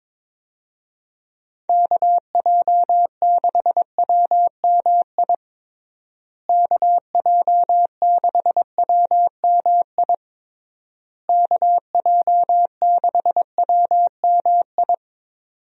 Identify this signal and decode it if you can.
Morse code, heard as a keyed tone, about 22 wpm, 700 Hz. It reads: KJ6WMI KJ6WMI KJ6WMI
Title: USA Call Signs - 6 Characters - Repeated 3x - 22wpm